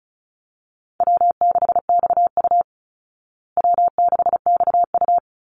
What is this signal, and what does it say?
Morse code, heard as a keyed tone, about 34 wpm, 700 Hz. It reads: W6XU W6XU
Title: Call Signs Repeated 3x - 35wpm